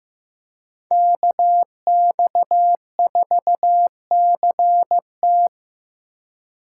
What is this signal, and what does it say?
Morse code, heard as a keyed tone, about 15 wpm, 700 Hz. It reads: KX4CT
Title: USA Call Signs - 5 Characters - 3x Character Spacing - 15wpm